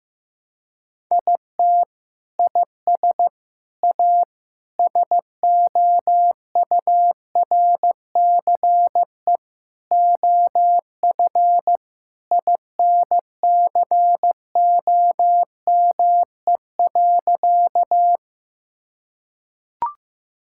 Morse code, heard as a keyed tone, about 15 wpm, 700 Hz. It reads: IT IS A SOURCE OF INCOME.
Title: Sentences from Top 1000 Words - Review Part 5 - 15wpm